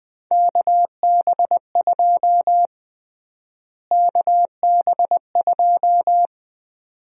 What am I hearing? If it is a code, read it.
Morse code, heard as a keyed tone, about 20 wpm, 700 Hz. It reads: KB2 KB2
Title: USA Call Sign Prefixes - Repeated 2x - 20wpm